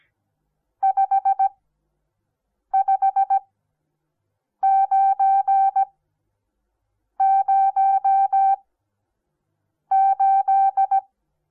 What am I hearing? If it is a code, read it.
Morse code, heard as a keyed tone, about 17 wpm, spaced as for 6 wpm, 780 Hz. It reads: 55908
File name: Onlinelehrgang / morsen / DJ4UF Morsen08